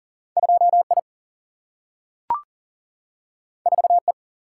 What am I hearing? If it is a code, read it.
Morse code, heard as a keyed tone, about 39 wpm, 700 Hz. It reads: 2I 4E
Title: Single-Letter Single-Number 40wpm